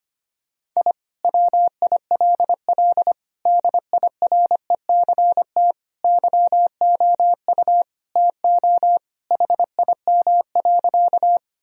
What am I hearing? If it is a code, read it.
Morse code, heard as a keyed tone, about 25 wpm, 700 Hz. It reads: I WILL DIRECT YOU TO HIM.
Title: Sentences from Top 1000 Words - Review Part 2 - 25wpm